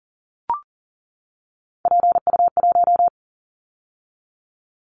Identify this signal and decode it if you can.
Morse code, heard as a keyed tone, about 39 wpm, 700 Hz. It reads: PU1
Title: Callsign Prefixes for Top 100 DXCC - 40wpm